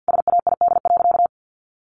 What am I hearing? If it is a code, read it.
Morse code, heard as a keyed tone, about 48 wpm, 700 Hz. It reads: SAID.
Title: Sentences from Top 1000 Words 50wpm